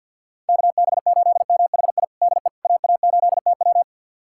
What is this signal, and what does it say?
Morse code, heard as a keyed tone, about 52 wpm, 700 Hz. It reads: KB8MHI DE AA8TW
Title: USA Callsign DE Callsign 50wpm